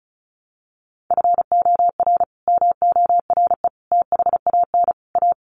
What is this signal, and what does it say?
Morse code, heard as a keyed tone, about 34 wpm, 700 Hz. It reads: FOR MORE THAN A
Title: Sets of 5 Words Encore2 35wpm